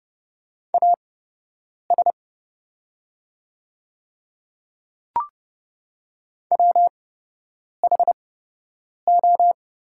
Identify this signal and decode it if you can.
Morse code, heard as a keyed tone, about 30 wpm, spaced as for 8 wpm, 700 Hz. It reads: AS WHO
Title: 8x Character Spacing - Top 100 Words - 30wpm